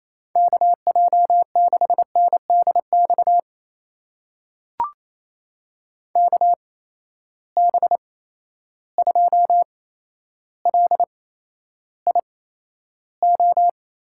Morse code, heard as a keyed tone, about 28 wpm, spaced as for 12 wpm, 700 Hz. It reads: KJ6NDX KB2LIO
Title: USA Call Signs - 6 Characters - 8x Character Spacing - 28wpm